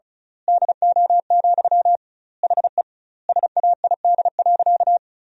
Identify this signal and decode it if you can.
Morse code, heard as a keyed tone, about 34 wpm, 700 Hz. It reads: DO, HE SAID.
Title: 2x Word-Spacing - Sentences from Top 100 Words - 35wpm